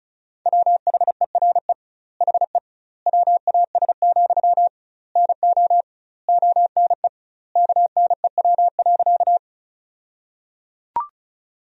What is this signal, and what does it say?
Morse code, heard as a keyed tone, about 36 wpm, 700 Hz. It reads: WHERE HE WAS, NO ONE KNEW.
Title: 2x Word-Spacing - Sentences from Top 400 Words - 35wpm